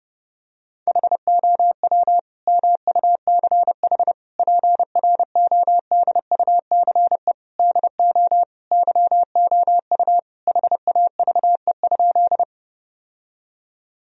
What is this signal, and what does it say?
Morse code, heard as a keyed tone, about 32 wpm, 700 Hz. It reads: HOW MUCH PRODUCE DO YOU HAVE?
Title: Sentences from Top 1000 Words - Review Part 3 - 30wpm